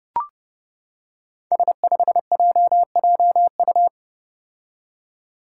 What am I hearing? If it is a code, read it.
Morse code, heard as a keyed tone, about 29 wpm, 700 Hz. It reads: S5JJU
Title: Five Challenging Letter-Number Combinations 30wpm